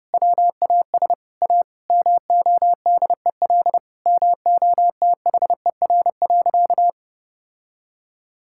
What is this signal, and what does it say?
Morse code, heard as a keyed tone, about 30 wpm, 700 Hz. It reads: WAS A MODEL MOTHER.
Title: Sentences from Top 1000 Words - Review Part 5 - 30wpm